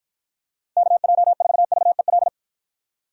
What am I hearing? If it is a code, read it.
Morse code, heard as a keyed tone, about 53 wpm, 700 Hz. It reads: KJ4VEL